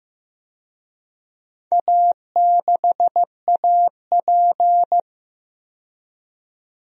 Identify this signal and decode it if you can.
Morse code, heard as a keyed tone, about 15 wpm, 700 Hz. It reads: A6AP
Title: DXCC-100 Call Signs - 4 Characters - 15wpm